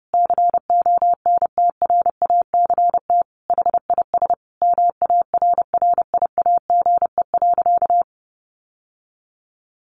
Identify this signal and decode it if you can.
Morse code, heard as a keyed tone, about 30 wpm, 700 Hz. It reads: CONTRACT HIS MARRIAGE.
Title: Sentences from Top 1400 Words 30wpm